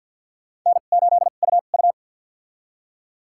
Morse code, heard as a keyed tone, about 52 wpm, spaced as for 36 wpm, 700 Hz. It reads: N8UU
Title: USA Call Signs - 4 Characters - 2x Character Spacing - 50wpm